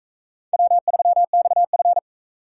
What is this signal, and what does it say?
Morse code, heard as a keyed tone, about 44 wpm, 700 Hz. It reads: W3XF